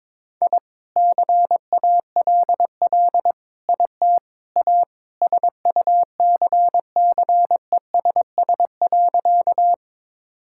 Morse code, heard as a keyed tone, about 22 wpm, 700 Hz. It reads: I CALL IT A SUCCESS.
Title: Sentences from Top 1000 Words - Review Part 6 - 22wpm